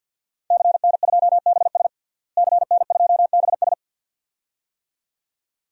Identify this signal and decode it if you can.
Morse code, heard as a keyed tone, about 49 wpm, 700 Hz. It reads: CN2BS CN2BS
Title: DXCC-100 Call Signs - 5 Characters - Repeated 2x - 50wpm